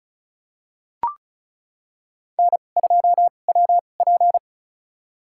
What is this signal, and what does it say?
Morse code, heard as a keyed tone, about 36 wpm, spaced as for 25 wpm, 700 Hz. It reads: N2WP
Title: USA Call Signs - 2x Character Spacing - 35wpm